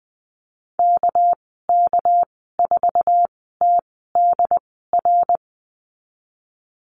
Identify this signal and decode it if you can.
Morse code, heard as a keyed tone, about 19 wpm, spaced as for 14 wpm, 700 Hz. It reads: KK4TDR
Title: 207 Lesson 33g - Letter j - Call Signs 20wpm